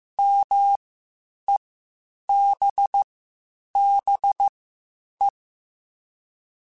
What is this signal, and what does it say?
Morse code, heard as a keyed tone, about 15 wpm, spaced as for 8 wpm, 800 Hz. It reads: MEBBE
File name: (spaced) / QSOs Spaced-mp3s / WA7QOX 1 @13 sp